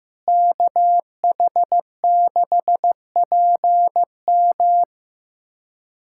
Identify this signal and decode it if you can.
Morse code, heard as a keyed tone, about 15 wpm, 700 Hz. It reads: KH6PM
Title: USA Callsign DE Callsign - 5 Characters - 15wpm